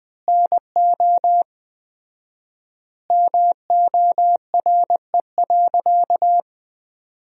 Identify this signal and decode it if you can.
Morse code, heard as a keyed tone, about 20 wpm, 700 Hz. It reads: NO MORE.
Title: 4x Word-Spacing - Sentences from Top 400 Words - 20wpm